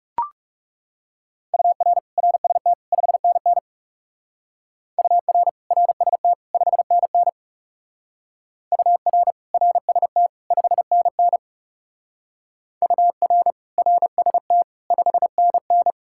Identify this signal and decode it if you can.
Morse code, heard as a keyed tone, about 40 wpm, 700 Hz. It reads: UR RST 5NN UR RST 5NN UR RST 5NN UR RST 5NN